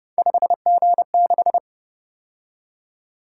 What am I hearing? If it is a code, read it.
Morse code, heard as a keyed tone, about 29 wpm, 700 Hz. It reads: 5G6
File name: Speed-Racing - Three Letter-Number Combinations - 25wpm